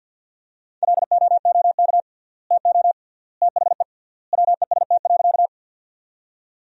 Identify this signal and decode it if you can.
Morse code, heard as a keyed tone, about 50 wpm, 700 Hz. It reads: LOOK TO THE WEST.
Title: 3x Word-Spacing - Sentences from Top 400 Words - 50wpm